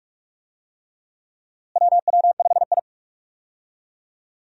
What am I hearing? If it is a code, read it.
Morse code, heard as a keyed tone, about 41 wpm, 700 Hz. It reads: WW5I
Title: USA Call Signs - 4 Characters - Repeated 3x - 45wpm